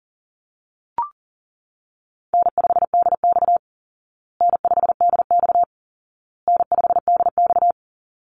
Morse code, heard as a keyed tone, about 40 wpm, 700 Hz. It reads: N5DX N5DX N5DX
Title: Easy Call Signs Repeated 3x - 40wpm